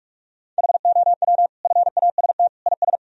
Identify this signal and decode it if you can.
Morse code, heard as a keyed tone, about 43 wpm, 700 Hz. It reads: HOW FAST IS
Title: Sentences from Top 1000 Words - Review Part 4 - 45wpm